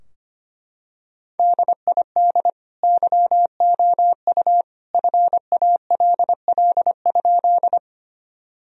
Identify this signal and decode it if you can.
Morse code, heard as a keyed tone, about 25 wpm, 700 Hz. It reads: DID YOU FALL?
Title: Sentences from Top 1000 Words - Review Part 2 - 25wpm